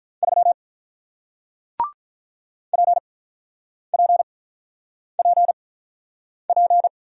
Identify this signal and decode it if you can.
Morse code, heard as a keyed tone, about 48 wpm, spaced as for 9 wpm, 700 Hz. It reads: 3 PPPP